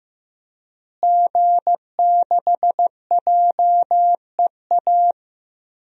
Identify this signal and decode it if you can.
Morse code, heard as a keyed tone, about 15 wpm, 700 Hz. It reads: G6JEA
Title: DXCC-100 Call Signs - 5 Characters - Repeated 3x - 15wpm